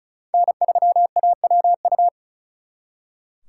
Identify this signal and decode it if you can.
Morse code, heard as a keyed tone, about 46 wpm, 700 Hz. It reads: N3AWU